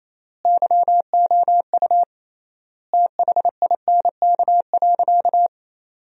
Morse code, heard as a keyed tone, about 28 wpm, 700 Hz. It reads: YOU THINK.
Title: 3x Word-Spacing - Sentences from Top 400 Words - 28wpm